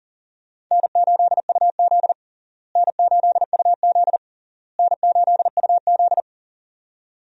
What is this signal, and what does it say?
Morse code, heard as a keyed tone, about 40 wpm, 700 Hz. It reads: N8UZ N8UZ N8UZ
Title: USA Call Signs - 4 Characters - Repeated 3x - 40wpm